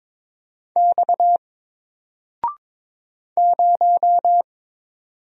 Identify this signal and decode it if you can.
Morse code, heard as a keyed tone, about 22 wpm, 700 Hz. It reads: X 0
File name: Single-Letter-Number-22wpm